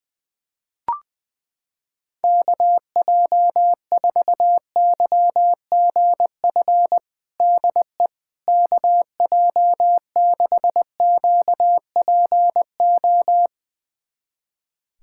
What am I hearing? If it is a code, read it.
Morse code, heard as a keyed tone, about 20 wpm, 700 Hz. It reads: KJ4YGF DE KJ6QPO
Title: USA Callsign DE Callsign 20wpm